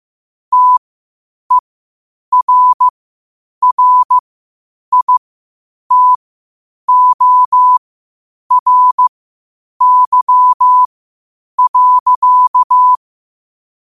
Morse code, heard as a keyed tone, about 15 wpm, spaced as for 8 wpm, 1000 Hz. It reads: TERRITORY.
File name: qotdfiles / 8wpm